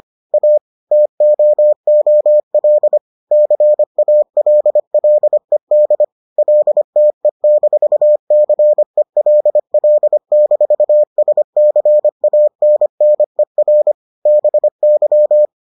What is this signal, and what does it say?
Morse code, heard as a keyed tone, about 25 wpm, 600 Hz. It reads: A TOOL CALLED LTE-CELL-SCANNER BY